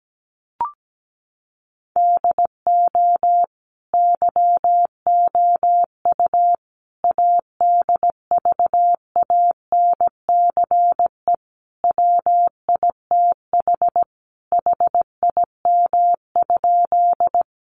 Morse code, heard as a keyed tone, about 17 wpm, 700 Hz. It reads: DO YOU ADVANCE WITH HIM?
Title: Sentences from Top 1400 Words 17wpm